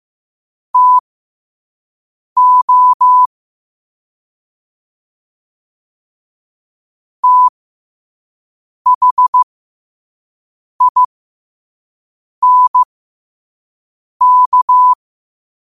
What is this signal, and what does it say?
Morse code, heard as a keyed tone, about 15 wpm, spaced as for 5 wpm, 1000 Hz. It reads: TO THINK
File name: qotdfiles / 5wpm